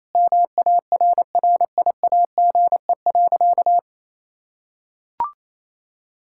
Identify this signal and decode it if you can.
Morse code, heard as a keyed tone, about 28 wpm, 700 Hz. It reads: MARRIAGE.
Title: Sentences from Top 1000 Words - Review Part 5 - 28wpm